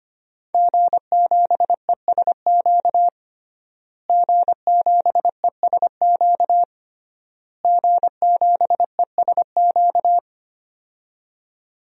Morse code, heard as a keyed tone, about 25 wpm, 700 Hz. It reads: G7ESQ G7ESQ G7ESQ
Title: DXCC-100 Call Signs - 5 Characters - Repeated 3x - 25wpm